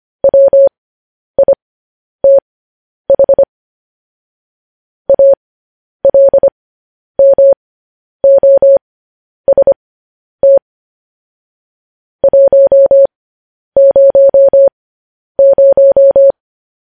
Morse code, heard as a keyed tone, about 25 wpm, spaced as for 10 wpm, 550 Hz. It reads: WITH ALMOST 100
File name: podcast / News headlines-1x-25x10